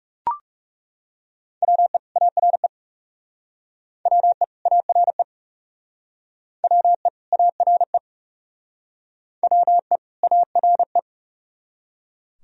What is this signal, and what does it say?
Morse code, heard as a keyed tone, about 37 wpm, 700 Hz. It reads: WE ARE WE ARE WE ARE WE ARE
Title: Speed-Racing - Sets of 2 Words 30wpm